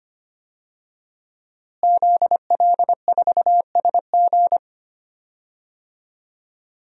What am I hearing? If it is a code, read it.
Morse code, heard as a keyed tone, about 25 wpm, 700 Hz. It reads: ZL4SG
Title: DXCC-100 Call Signs - 5 Characters - Repeated 3x - 25wpm